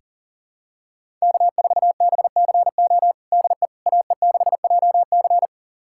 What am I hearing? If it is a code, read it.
Morse code, heard as a keyed tone, about 40 wpm, 700 Hz. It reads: K4BCO DE AE6JC
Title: USA Callsign DE Callsign - 5 Characters - 40wpm